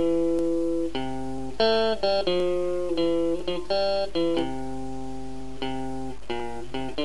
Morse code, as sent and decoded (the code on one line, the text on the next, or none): none